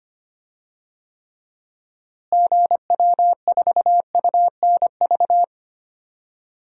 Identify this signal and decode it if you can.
Morse code, heard as a keyed tone, about 25 wpm, 700 Hz. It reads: GW4UNV
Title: DXCC-100 Call Signs - 6 Characters - Repeated 3x - 25wpm